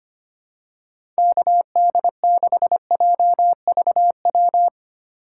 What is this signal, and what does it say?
Morse code, heard as a keyed tone, about 25 wpm, 700 Hz. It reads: KD6JVW
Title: USA Call Signs - 6 Characters - 8x Character Spacing - 25wpm